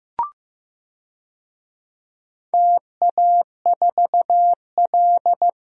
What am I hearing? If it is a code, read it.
Morse code, heard as a keyed tone, about 15 wpm, 700 Hz. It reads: TA4L
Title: DXCC-100 Call Signs - 4 Characters - 15wpm